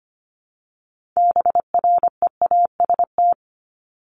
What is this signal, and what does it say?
Morse code, heard as a keyed tone, about 25 wpm, 700 Hz. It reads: BREAST
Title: Top 1751-2000 Words 25wpm